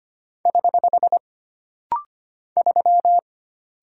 Morse code, heard as a keyed tone, about 25 wpm, 700 Hz. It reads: <HH> 3
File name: Single Character (Letter:Number:Punctuation:Prosign) - Rapid-Fire - 25wpm